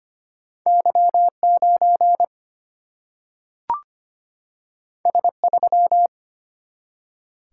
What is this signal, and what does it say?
Morse code, heard as a keyed tone, about 25 wpm, 700 Hz. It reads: Y9 S3
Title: Single-Letter Single-Number 25wpm